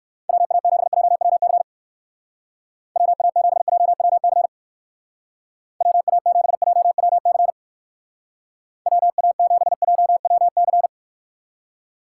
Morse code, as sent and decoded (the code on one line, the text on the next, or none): .-- .- --... .--- .-- -.-. / .-- .- --... .--- .-- -.-. / .-- .- --... .--- .-- -.-. / .-- .- --... .--- .-- -.-.
WA7JWC WA7JWC WA7JWC WA7JWC